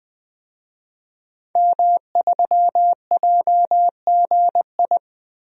Morse code, heard as a keyed tone, about 20 wpm, 700 Hz. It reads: M3JGI
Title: DXCC-100 Call Signs - 5 Characters - Repeated 2x - 20wpm